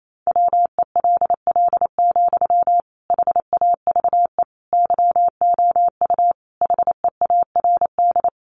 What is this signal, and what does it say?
Morse code, heard as a keyed tone, about 28 wpm, 700 Hz. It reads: WELL, HAVE YOU HEARD
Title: Sentences from Top 800 Words 28wpm